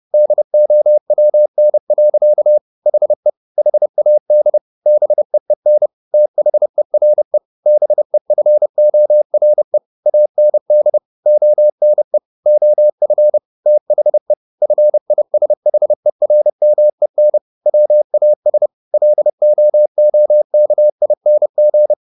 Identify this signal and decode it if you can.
Morse code, heard as a keyed tone, about 30 wpm, 600 Hz. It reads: DOWN. HE HAD BEEN THERE BEFORE AND ONE OF THE FISHERMEN WAS LOOKING